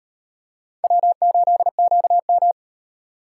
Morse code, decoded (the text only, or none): W8QM